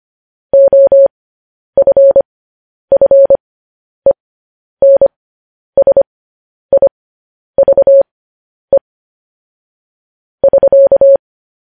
Morse code, heard as a keyed tone, about 25 wpm, spaced as for 10 wpm, 550 Hz. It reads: OFFENSIVE <SK>
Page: News Headlines in Morse Code at 10 WPM